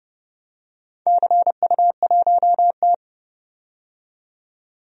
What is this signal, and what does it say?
Morse code, heard as a keyed tone, about 30 wpm, 700 Hz. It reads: CU1T
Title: DXCC-100 Call Signs - 4 Characters - 30wpm